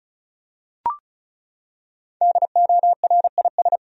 Morse code, heard as a keyed tone, about 34 wpm, 700 Hz. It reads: DORIS
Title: USA Common Names - 35wpm